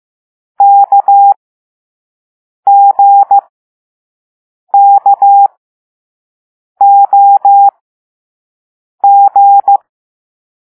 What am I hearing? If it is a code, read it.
Morse code, heard as a keyed tone, about 15 wpm, spaced as for 5 wpm, 800 Hz. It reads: KGKOG
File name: Morse / Level4A